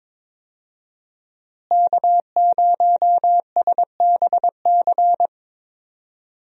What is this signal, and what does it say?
Morse code, heard as a keyed tone, about 22 wpm, 700 Hz. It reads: K0SBC
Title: USA Call Signs - 5 Characters - Repeated 3x - 22wpm